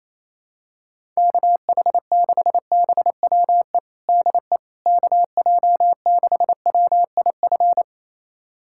Morse code, heard as a keyed tone, about 28 wpm, 700 Hz. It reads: KH6BWE DE KJ6WIF
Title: USA Callsign DE Callsign 28wpm